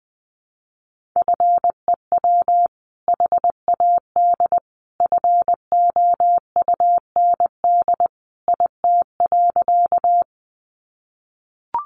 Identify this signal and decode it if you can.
Morse code, heard as a keyed tone, about 20 wpm, 700 Hz. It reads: FEW HAD FOUND IT.
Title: Sentences from Top 300 Words 20wpm